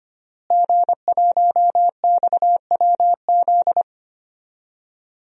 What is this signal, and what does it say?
Morse code, heard as a keyed tone, about 25 wpm, 700 Hz. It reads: G1XWZ
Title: DXCC-100 Call Signs - 5 Characters - Repeated 3x - 25wpm